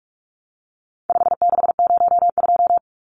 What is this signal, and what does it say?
Morse code, heard as a keyed tone, about 44 wpm, 700 Hz. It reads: 5602
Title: Numbers - 4-Digit 45wpm